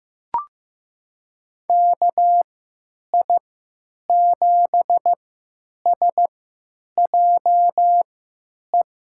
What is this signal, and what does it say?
Morse code, heard as a keyed tone, about 15 wpm, spaced as for 8 wpm, 700 Hz. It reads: KI7SJE
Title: USA Call Signs - 6 Characters - 3x Character Spacing - 15wpm